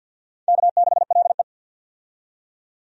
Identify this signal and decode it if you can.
Morse code, heard as a keyed tone, about 47 wpm, 700 Hz. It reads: K6LE
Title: USA Call Signs - 4 Characters - 50wpm